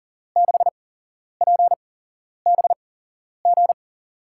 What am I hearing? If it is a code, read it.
Morse code, heard as a keyed tone, about 40 wpm, spaced as for 11 wpm, 700 Hz. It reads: 6PBG
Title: USA Call Signs - 6 Characters - 8x Character Spacing - 40wpm